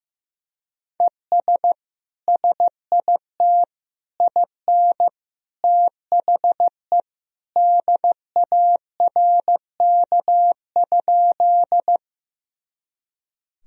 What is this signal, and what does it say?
Morse code, heard as a keyed tone, about 15 wpm, 700 Hz. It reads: ES SIT IN THE DARK?
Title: Sentences from Top 500 Words and Abbreviations 15wpm